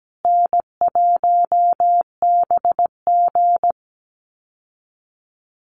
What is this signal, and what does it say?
Morse code, heard as a keyed tone, about 17 wpm, 700 Hz. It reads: N1BG
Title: Call Signs Repeated 2x - 17wpm